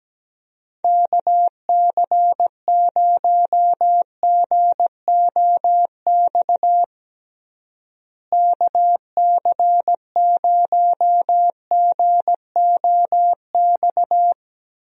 Morse code, heard as a keyed tone, about 17 wpm, 700 Hz. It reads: KC0GOX KC0GOX
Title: USA Call Signs - 6 Characters - Repeated 3x - 17wpm